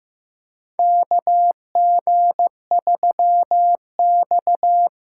Morse code, heard as a keyed tone, about 15 wpm, 700 Hz. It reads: KG3X
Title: USA Call Signs - 4x Character Spacing - 15wpm